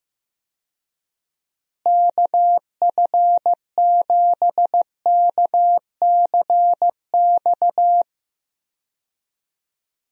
Morse code, heard as a keyed tone, about 15 wpm, 700 Hz. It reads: KF7KCX
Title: USA Call Signs - 6 Characters - Repeated 3x - 15wpm